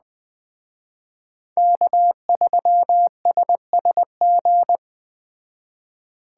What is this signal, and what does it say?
Morse code, heard as a keyed tone, about 20 wpm, 700 Hz. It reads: K3SSG
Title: Speed Racing - USA 1x3 Call Signs 17wpm